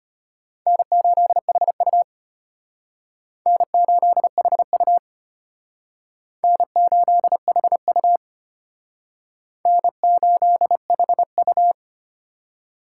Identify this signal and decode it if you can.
Morse code, heard as a keyed tone, about 34 wpm, 700 Hz. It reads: N8HU N8HU N8HU N8HU